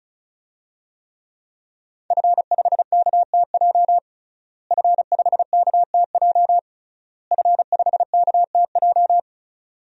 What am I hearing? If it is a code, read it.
Morse code, heard as a keyed tone, about 36 wpm, 700 Hz. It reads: F5KTJ F5KTJ F5KTJ
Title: DXCC-100 Call Signs - 5 Characters - Repeated 3x - 35wpm